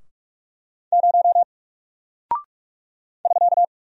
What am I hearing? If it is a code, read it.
Morse code, heard as a keyed tone, about 44 wpm, 700 Hz. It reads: 0 <SK>
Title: Single Character (Letter:Number:Punctuation:Prosign) Alt 45wpm